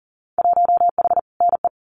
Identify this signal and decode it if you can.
Morse code, heard as a keyed tone, about 37 wpm, 700 Hz. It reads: 1H NE
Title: Winter Field Day Exchanges 40wpm